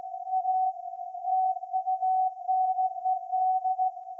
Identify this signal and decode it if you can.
Morse code, heard as a keyed tone, about 16 wpm, 750 Hz. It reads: A TUNED
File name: iz7slz / VLF / SAQ 30-06-2024 0900